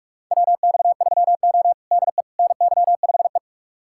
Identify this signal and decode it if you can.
Morse code, heard as a keyed tone, about 47 wpm, 700 Hz. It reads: WX3O DE NY5E
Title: USA Callsign DE Callsign - 4 Characters - 45wpm